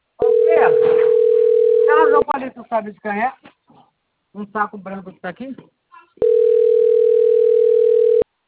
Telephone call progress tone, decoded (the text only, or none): ringback tone